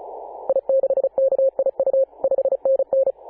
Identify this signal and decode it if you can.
Morse code, heard as a keyed tone, about 34 wpm, 540 Hz. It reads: I6KIU 5NN
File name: Apr/29 2042z JD1BLY 14.011MHz CW